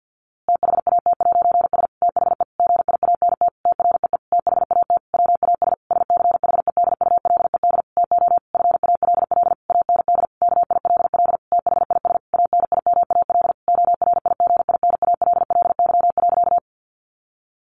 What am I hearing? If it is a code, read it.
Morse code, heard as a keyed tone, about 48 wpm, 700 Hz. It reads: THAT'S THE GIANT TREE THAT WAS SCHEDULED TO FALL AND KILL THIS ANIMAL ORIGINALLY.
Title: A Sound Of Thunder 50wpm